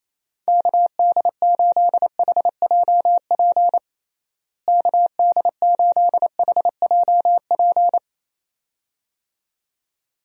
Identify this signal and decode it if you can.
Morse code, heard as a keyed tone, about 28 wpm, 700 Hz. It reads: KD8HJP KD8HJP
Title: USA Call Signs - 6 Characters - Repeated 2x - 28wpm